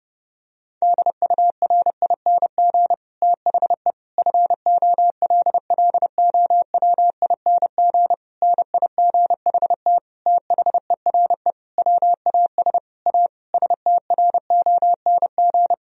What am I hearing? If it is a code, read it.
Morse code, heard as a keyed tone, about 32 wpm, 700 Hz. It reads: DURING THE FOLLOWING NIGHT THERE WAS A STRONG
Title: Sentences from Top 1000 Words - Review Part 4 - 30wpm